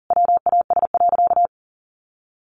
Morse code, heard as a keyed tone, about 40 wpm, 700 Hz. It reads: WAS.
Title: Sentences from Top 1400 Words 40wpm